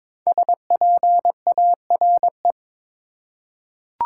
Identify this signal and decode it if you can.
Morse code, heard as a keyed tone, about 22 wpm, 700 Hz. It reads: SPARE
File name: 5 Letter Words - First 3 Letters Given - 22wpm